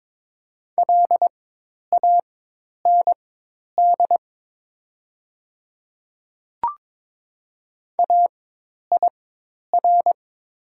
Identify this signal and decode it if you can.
Morse code, heard as a keyed tone, about 22 wpm, spaced as for 10 wpm, 700 Hz. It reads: LAND AIR
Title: 4x Character Spacing - Top 500 Words - 22wpm